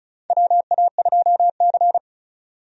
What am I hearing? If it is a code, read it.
Morse code, heard as a keyed tone, about 36 wpm, 700 Hz. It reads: WA2C
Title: USA Call Signs - 4 Characters - 35wpm